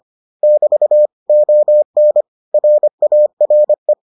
Morse code, heard as a keyed tone, about 25 wpm, 600 Hz. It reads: <BT> ON RARE